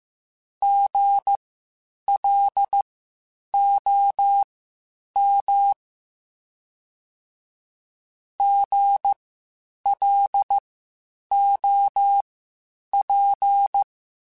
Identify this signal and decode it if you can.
Morse code, heard as a keyed tone, about 15 wpm, spaced as for 8 wpm, 800 Hz. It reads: GLOM GLOP
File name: Spaced Alphas / Spaced Alphas mp3s / Gs sp @13